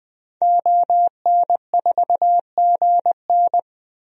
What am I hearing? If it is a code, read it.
Morse code, heard as a keyed tone, about 20 wpm, 700 Hz. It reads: ON4GN
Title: DXCC-100 Call Signs - 5 Characters - 20wpm